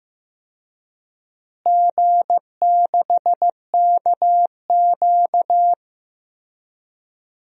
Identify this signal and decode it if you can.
Morse code, heard as a keyed tone, about 15 wpm, 700 Hz. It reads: G6KQ
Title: DXCC-100 Call Signs - 4 Characters - Repeated 3x - 15wpm